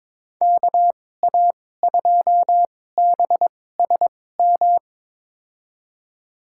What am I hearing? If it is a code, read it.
Morse code, heard as a keyed tone, about 22 wpm, spaced as for 16 wpm, 700 Hz. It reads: KA2BSM
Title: USA Call Signs - 2x Character Spacing - 22wpm